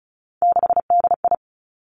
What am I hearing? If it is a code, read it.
Morse code, heard as a keyed tone, about 36 wpm, 700 Hz. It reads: 6DI
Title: Three Challenging Letter-Number Combinations 35wpm